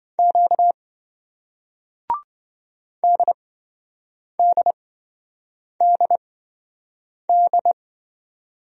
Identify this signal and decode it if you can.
Morse code, heard as a keyed tone, about 26 wpm, spaced as for 7 wpm, 700 Hz. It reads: Q DDDD